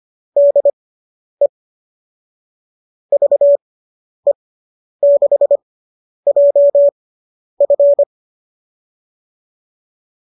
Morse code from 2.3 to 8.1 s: VE6JF

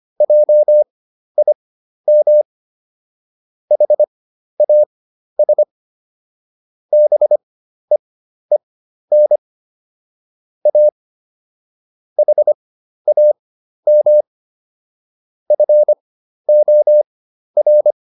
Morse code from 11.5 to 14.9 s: HAM

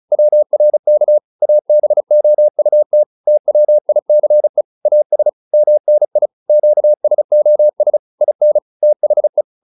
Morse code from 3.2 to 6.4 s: TWICE AS MNI

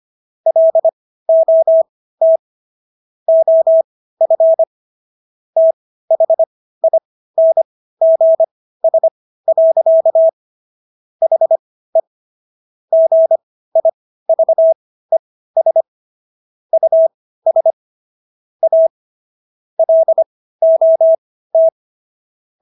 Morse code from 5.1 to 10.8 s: THINGS.

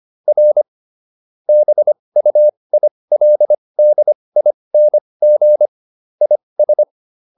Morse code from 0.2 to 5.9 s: R BUILDING

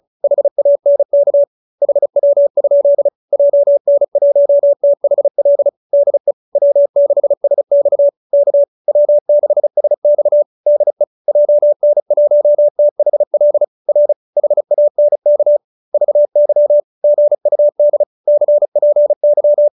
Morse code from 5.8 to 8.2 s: DE W6SX

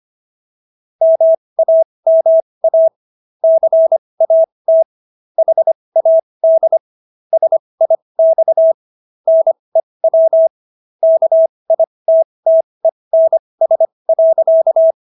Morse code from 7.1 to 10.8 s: SIX NEW

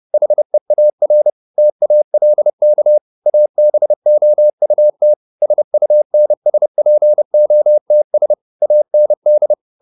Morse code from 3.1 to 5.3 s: ABOUT

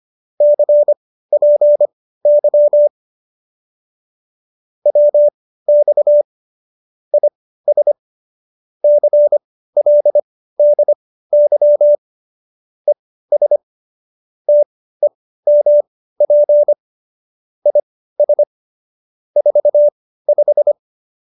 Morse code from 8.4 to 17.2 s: CLDY ES TEMP

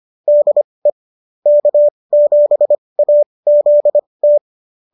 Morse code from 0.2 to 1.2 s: DE

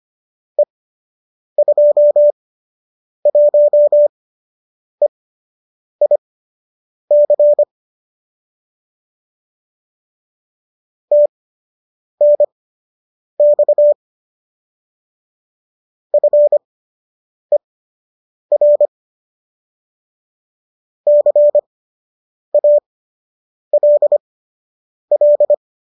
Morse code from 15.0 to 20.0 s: FER

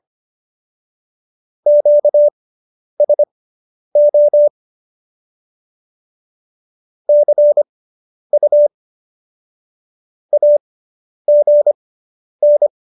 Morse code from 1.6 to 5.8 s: QSO

